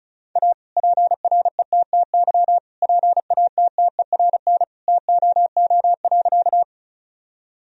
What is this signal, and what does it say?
Morse code, heard as a keyed tone, about 34 wpm, 700 Hz. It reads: A PRETTY PATTERN TOO.
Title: Sentences from Top 1000 Words - Review Part 2 - 35wpm